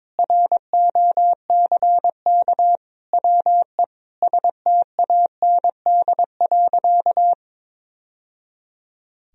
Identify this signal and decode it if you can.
Morse code, heard as a keyed tone, about 22 wpm, 700 Hz. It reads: ROCK WE STAND.
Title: Sentences from Top 1000 Words - Review Part 2 - 22wpm